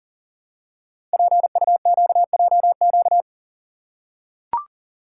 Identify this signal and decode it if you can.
Morse code, heard as a keyed tone, about 39 wpm, 700 Hz. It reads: PUQJQ
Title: Code Groups - 5 Letters - 40wpm